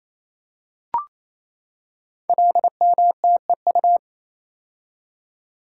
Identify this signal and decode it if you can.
Morse code, heard as a keyed tone, about 27 wpm, 700 Hz. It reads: LMTEU
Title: Code Groups - 5 Letters - 28wpm